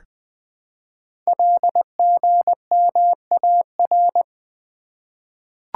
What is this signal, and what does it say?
Morse code, heard as a keyed tone, about 20 wpm, 700 Hz. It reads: LGMAR
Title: Code Groups - 5 Letters - 20wpm